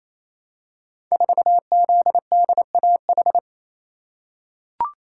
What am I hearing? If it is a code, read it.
Morse code, heard as a keyed tone, about 27 wpm, 700 Hz. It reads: 4ZDAH
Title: Code Groups - 5 Letter:Numbers - 28wpm